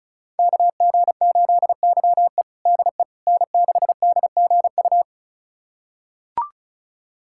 Morse code, heard as a keyed tone, about 36 wpm, 690 Hz. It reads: KG8YE DE N6DGU
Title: USA Callsign DE Callsign - 5 Characters - 35wpm